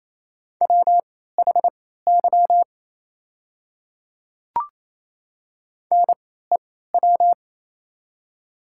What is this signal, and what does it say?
Morse code, heard as a keyed tone, about 28 wpm, spaced as for 16 wpm, 700 Hz. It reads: WHY NEW
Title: 3x Character Spacing - Top 300 Words - 28wpm